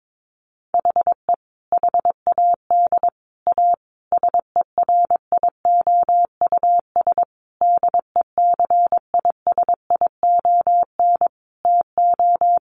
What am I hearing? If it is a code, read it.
Morse code, heard as a keyed tone, about 22 wpm, 700 Hz. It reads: HE HAD A SERIOUS DECISION TO
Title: Sentences from Top 1100 Words 22wpm